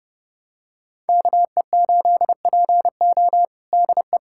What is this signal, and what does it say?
Morse code, heard as a keyed tone, about 30 wpm, 700 Hz. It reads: KE8PO DE
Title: USA Callsign DE Callsign - 5 Characters - 30wpm